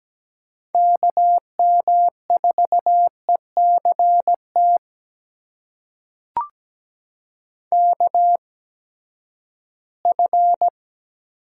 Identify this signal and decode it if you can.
Morse code, heard as a keyed tone, about 17 wpm, spaced as for 12 wpm, 700 Hz. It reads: KM4ECT KF
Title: USA Call Signs - 6 Characters - 8x Character Spacing - 17wpm